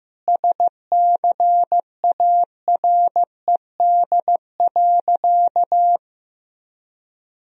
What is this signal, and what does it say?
Morse code, heard as a keyed tone, about 15 wpm, 700 Hz. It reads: SCARED.
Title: Sentences from Top 1000 Words - Review Part 4 - 15wpm